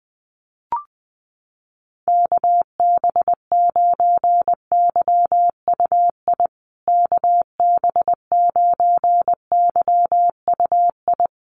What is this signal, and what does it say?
Morse code, heard as a keyed tone, about 20 wpm, 700 Hz. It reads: KB9YUI KB9YUI
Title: USA Call Signs Repeated 3x 20wpm